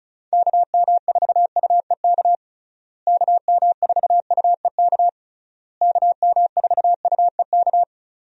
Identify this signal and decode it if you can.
Morse code, heard as a keyed tone, about 36 wpm, 700 Hz. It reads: KM4UEK KM4UEK KM4UEK
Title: USA Call Signs - 6 Characters - Repeated 3x - 35wpm